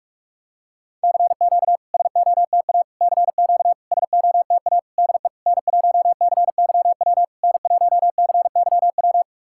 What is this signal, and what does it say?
Morse code, heard as a keyed tone, about 44 wpm, 690 Hz. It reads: CQ SOTA CQ SOTA DE N1CYW N1CYW
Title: SOTA QSO Practice 45wpm